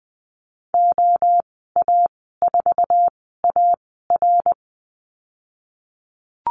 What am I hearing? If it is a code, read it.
Morse code, heard as a keyed tone, about 19 wpm, spaced as for 14 wpm, 700 Hz. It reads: OA4AR
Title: 173 Lesson 28g - Number 9 - Call Signs 20wpm